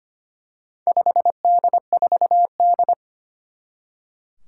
Morse code, heard as a keyed tone, about 25 wpm, 700 Hz. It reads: 5D4D
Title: Four Challenging Letter-Number Combinations 25wpm